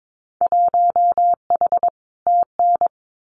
Morse code, heard as a keyed tone, about 22 wpm, 700 Hz. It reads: 1H TN
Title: Winter Field Day Exchanges 22wpm